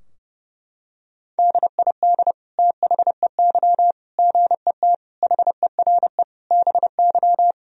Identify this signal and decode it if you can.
Morse code, heard as a keyed tone, about 30 wpm, 700 Hz. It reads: DID THEY GET HERE BY
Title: Sentences from Top 1000 Words - Review Part 3 - 30wpm